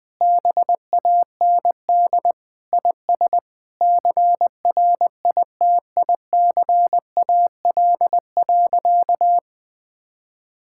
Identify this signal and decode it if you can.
Morse code, heard as a keyed tone, about 20 wpm, 700 Hz. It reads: BAND IS CRITICAL.
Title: Sentences from Top 1600 Words 20wpm